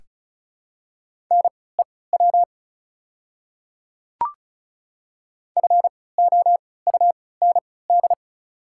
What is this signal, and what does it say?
Morse code, heard as a keyed tone, about 36 wpm, spaced as for 20 wpm, 700 Hz. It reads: NEW FOUND
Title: 3x Character Spacing - Top 500 Words - 35wpm